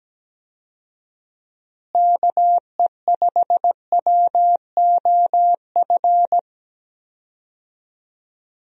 Morse code, heard as a keyed tone, about 17 wpm, 700 Hz. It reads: KE5WOF
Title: USA Call Signs - 6 Characters - Repeated 3x - 17wpm